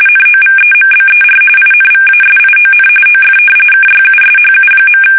RTTY, about 45.45 baud. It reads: NOVEMBER INVASION OF KYUSHU.
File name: fsk850 / battleshipnj